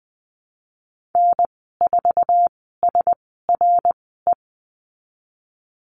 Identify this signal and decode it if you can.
Morse code, heard as a keyed tone, about 20 wpm, spaced as for 14 wpm, 700 Hz. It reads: N4SRE
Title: 122 Lesson 20g - Number 3 - Call Signs 20wpm